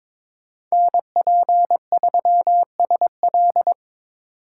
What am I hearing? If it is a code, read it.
Morse code, heard as a keyed tone, about 22 wpm, 700 Hz. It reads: NP3SL
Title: USA Call Signs - 5 Characters - 3x Character Spacing - 22wpm